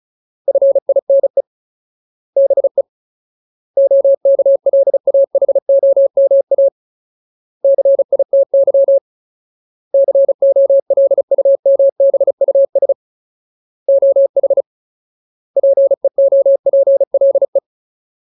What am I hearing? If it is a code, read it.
Morse code, heard as a keyed tone, about 36 wpm, 550 Hz. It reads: FINE BE OKLAHOMA CITY COLUMBUS OH PEOPLE
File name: Ultimate Word List / 35WPM Ultimate